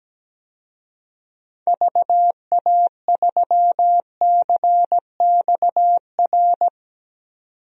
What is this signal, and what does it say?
Morse code, heard as a keyed tone, about 17 wpm, 700 Hz. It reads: VA3CXR
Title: DXCC-100 Call Signs - 6 Characters - 17wpm